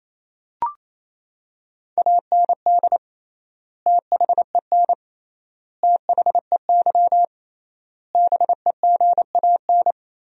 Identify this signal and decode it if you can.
Morse code, heard as a keyed tone, about 28 wpm, 700 Hz. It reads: AND THEN THEY BEGAN
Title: 3x Word-Spacing - Sentences from Top 300 Words - 28wpm